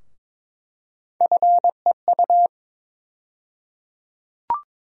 Morse code, heard as a keyed tone, about 22 wpm, 700 Hz. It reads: FEU
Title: USA Call Sign Postfixes - 3 Letters 22wpm